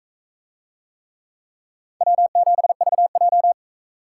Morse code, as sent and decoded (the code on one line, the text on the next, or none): .-- --... ...- .---
W7VJ